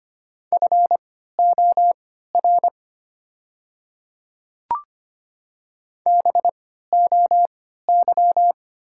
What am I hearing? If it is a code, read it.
Morse code, heard as a keyed tone, about 25 wpm, spaced as for 14 wpm, 690 Hz. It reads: FOR BOY
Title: 3x Character Spacing - Top 500 Words - 25wpm